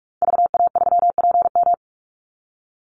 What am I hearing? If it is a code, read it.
Morse code, heard as a keyed tone, about 44 wpm, 700 Hz. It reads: VA3PM
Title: Call Signs - 45wpm